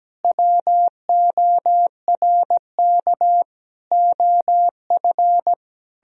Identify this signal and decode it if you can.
Morse code, heard as a keyed tone, about 17 wpm, 700 Hz. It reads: WORK OF
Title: Sentences from Top 1000 Words - Review Part 2 - 17wpm